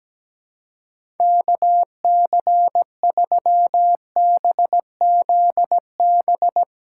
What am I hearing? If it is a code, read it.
Morse code, heard as a keyed tone, about 17 wpm, 700 Hz. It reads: KC3BZB
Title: USA Call Signs - 6 Characters - 4x Character Spacing - 17wpm